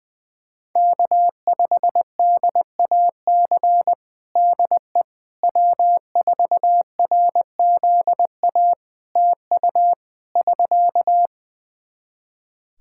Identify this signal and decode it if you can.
Morse code, heard as a keyed tone, about 20 wpm, 700 Hz. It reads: K5DAC DE W4RZA TU <SK>
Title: Practice QSO - Exchange 4 - Ending - 20wpm